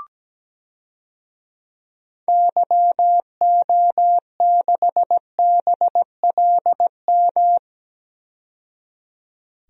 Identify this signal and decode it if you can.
Morse code, heard as a keyed tone, about 17 wpm, 700 Hz. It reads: YO6BLM
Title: DXCC-100 Call Signs - 6 Characters - 17wpm